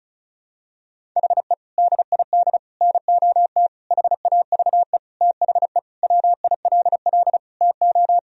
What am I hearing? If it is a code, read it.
Morse code, heard as a keyed tone, about 34 wpm, 700 Hz. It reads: HE DID NOT HAVE THE WILL TO
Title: Sentences from Top 1000 Words - Review Part 1 - 35wpm